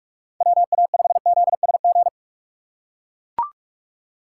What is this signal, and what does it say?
Morse code, heard as a keyed tone, about 43 wpm, 690 Hz. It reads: WA5ZSG
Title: USA Call Signs - 6 Characters - 3x Character Spacing - 45wpm